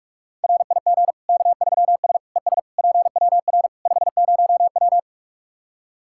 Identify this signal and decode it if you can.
Morse code, heard as a keyed tone, about 45 wpm, 700 Hz. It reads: RIG K3S ES PWR 50W
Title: Practice QSO - Exchange 2 - Info - 45wpm